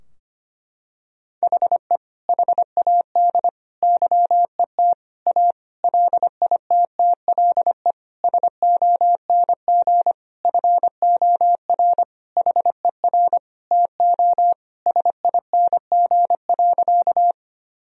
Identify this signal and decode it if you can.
Morse code, heard as a keyed tone, about 25 wpm, 700 Hz. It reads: HE HAD YET A LITTLE SONG FOR HER TO SING.
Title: Sentences from Top 1000 Words - Review Part 2 - 25wpm